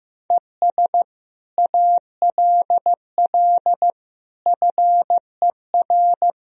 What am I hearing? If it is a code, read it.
Morse code, heard as a keyed tone, about 15 wpm, 700 Hz. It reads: ES ALL FER
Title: Sentences from Top 100 Words and Abbreviations 15wpm